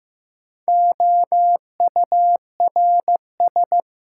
Morse code, heard as a keyed tone, about 15 wpm, 700 Hz. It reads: OURS
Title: Sentences from Top 1300 Words - New Word Drill Alt - 15wpm